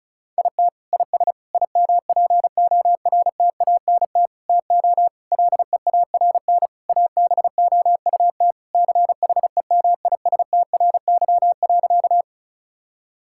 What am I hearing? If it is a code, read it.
Morse code, heard as a keyed tone, about 36 wpm, 700 Hz. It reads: IT IS IMPORTANT TO LEARN ABOUT CHEMISTRY.
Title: Sentences from Top 1000 Words - Review Part 6 - 35wpm